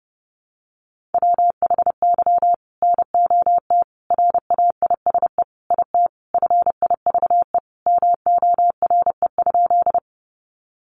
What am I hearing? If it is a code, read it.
Morse code, heard as a keyed tone, about 30 wpm, 700 Hz. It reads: WHY NOT RAISE IT FIVE MORE?
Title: Sentences from Top 1400 Words 30wpm